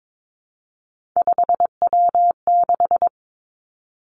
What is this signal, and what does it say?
Morse code, heard as a keyed tone, about 22 wpm, 700 Hz. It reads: 5W6
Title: Three Challenging Letter-Number Combinations 22wpm